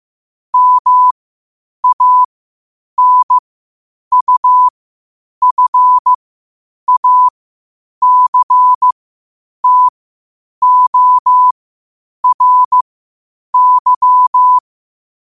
Morse code, heard as a keyed tone, about 15 wpm, spaced as for 8 wpm, 1000 Hz. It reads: MANUFACTORY
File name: qotdfiles / 8wpm